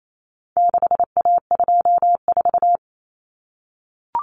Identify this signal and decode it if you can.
Morse code, heard as a keyed tone, about 28 wpm, 700 Hz. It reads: 6A24
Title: Common Cut Numbers - 4-Digit 28wpm